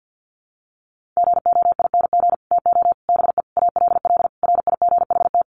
Morse code, heard as a keyed tone, about 50 wpm, 700 Hz. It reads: GOING TO BE ALL RIGHT
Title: Sets of 5 Words Encore2 50wpm